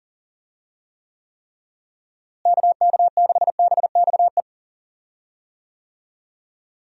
Morse code, heard as a keyed tone, about 40 wpm, 700 Hz. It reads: KK6BXE
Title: USA Call Signs - 6 Characters - Repeated 3x - 40wpm